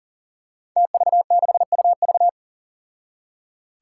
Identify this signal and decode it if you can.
Morse code, heard as a keyed tone, about 40 wpm, 700 Hz. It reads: TV6UV
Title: Five Challenging Letter-Number Combinations 40wpm